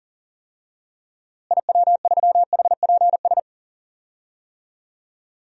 Morse code, heard as a keyed tone, about 37 wpm, 700 Hz. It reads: IW3HPS
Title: DXCC-100 Call Signs - 6 Characters - Repeated 2x - 40wpm